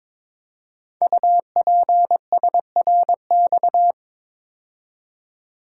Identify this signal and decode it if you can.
Morse code, heard as a keyed tone, about 22 wpm, 700 Hz. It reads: UPSRX
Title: Code Groups - 5 Letters - 22wpm